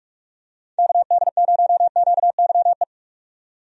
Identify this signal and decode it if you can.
Morse code, heard as a keyed tone, about 42 wpm, 700 Hz. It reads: KD0QYE
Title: USA Call Signs - 6 Characters - 45wpm